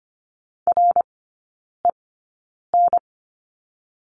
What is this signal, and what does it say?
Morse code, heard as a keyed tone, about 25 wpm, spaced as for 9 wpm, 690 Hz. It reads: REN
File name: Lesson 15e - Number 5 - Three Characters 25wpm